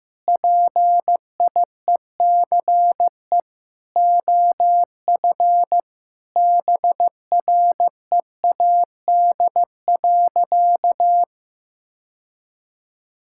Morse code, heard as a keyed tone, about 15 wpm, 700 Hz. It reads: PIECE OF BREAD.
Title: Sentences from Top 1600 Words 15wpm